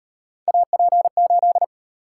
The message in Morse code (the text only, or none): AP8